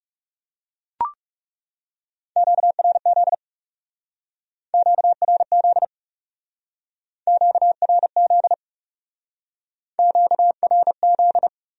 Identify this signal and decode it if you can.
Morse code, heard as a keyed tone, about 39 wpm, 700 Hz. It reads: QRZ QRZ QRZ QRZ